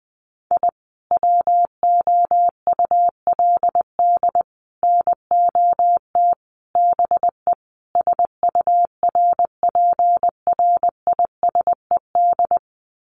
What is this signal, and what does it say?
Morse code, heard as a keyed tone, about 20 wpm, 700 Hz. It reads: I WOULD NOT BE SURPRISED
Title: Sets of 5 Words Encore2 20wpm